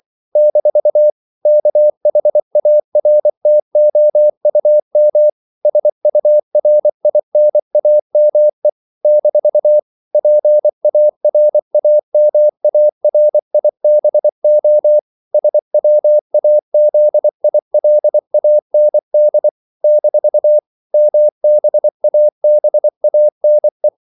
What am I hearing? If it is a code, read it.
Morse code, heard as a keyed tone, about 24 wpm, 600 Hz. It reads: - KHARTOUM SURINAME - PARAMARIBO SWAZILAND - MBABANE
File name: CW / Practice / NationalCapitals24